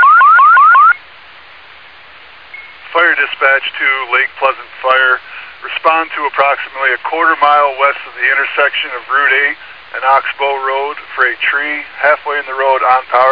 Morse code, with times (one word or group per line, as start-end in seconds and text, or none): none